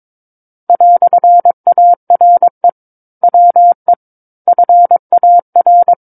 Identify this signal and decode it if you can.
Morse code, heard as a keyed tone, about 22 wpm, 700 Hz. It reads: "ARE WE FAR
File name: Book One / 13-Chapter 13 - 22 WPM